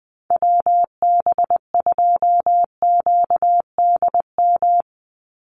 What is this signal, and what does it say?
Morse code, heard as a keyed tone, about 20 wpm, 700 Hz. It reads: WB2QDM
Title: USA Call Signs 20wpm